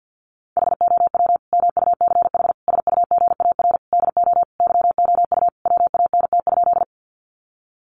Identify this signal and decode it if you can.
Morse code, heard as a keyed tone, about 50 wpm, 700 Hz. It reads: HOW MUCH SUGAR DO YOU WANT?
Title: Sentences from Top 1200 Words 50wpm